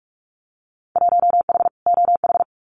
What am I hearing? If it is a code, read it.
Morse code, heard as a keyed tone, about 44 wpm, 700 Hz. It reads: 1H OH
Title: Winter Field Day Exchanges 45wpm